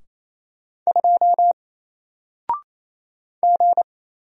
Morse code, heard as a keyed tone, about 28 wpm, 700 Hz. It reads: 2 G
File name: Single-Letter-Number-28wpm